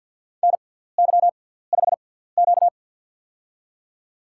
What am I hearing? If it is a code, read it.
Morse code, heard as a keyed tone, about 50 wpm, spaced as for 17 wpm, 700 Hz. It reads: NY5Q
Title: USA Call Signs - 4 Characters - 6x Character Spacing - 50wpm